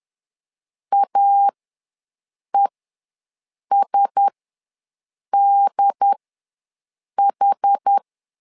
Morse code, heard as a keyed tone, about 10 wpm, spaced as for 6 wpm, 800 Hz. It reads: AESDH
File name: LEZIONI / 4 Q1S ws 10wpm 2208